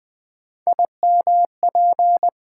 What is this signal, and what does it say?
Morse code, heard as a keyed tone, about 20 wpm, 700 Hz. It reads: IMP
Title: Build A Word - 3 Letters At A Time - 9-Letter Words - 20wpm